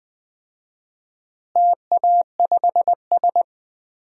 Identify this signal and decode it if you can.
Morse code, heard as a keyed tone, about 20 wpm, 700 Hz. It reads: TA5S
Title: DXCC-100 Call Signs - 4 Characters - Repeated 2x - 20wpm